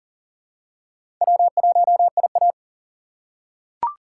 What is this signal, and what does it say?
Morse code, heard as a keyed tone, about 40 wpm, 700 Hz. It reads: W1IA
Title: USA Call Signs - 4 Characters - 3x Character Spacing - 40wpm